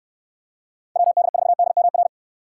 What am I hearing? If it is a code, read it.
Morse code, heard as a keyed tone, about 69 wpm, 700 Hz. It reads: WD4DMR